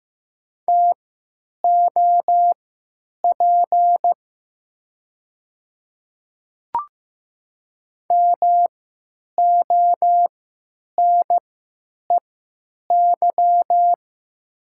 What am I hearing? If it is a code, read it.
Morse code, heard as a keyed tone, about 15 wpm, spaced as for 8 wpm, 700 Hz. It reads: TOP MONEY
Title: 3x Character Spacing - Top 500 Words - 15wpm